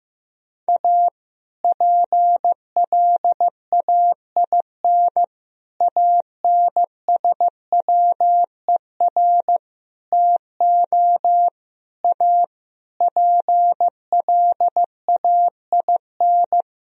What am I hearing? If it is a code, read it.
Morse code, heard as a keyed tone, about 15 wpm, 700 Hz. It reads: A PLAIN ANSWER TO A PLAIN
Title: Sentences from Top 1000 Words - Review Part 2 - 15wpm